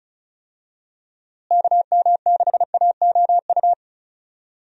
Morse code, heard as a keyed tone, about 36 wpm, 700 Hz. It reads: KM6AOU
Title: USA Call Signs - 6 Characters - Repeated 3x - 35wpm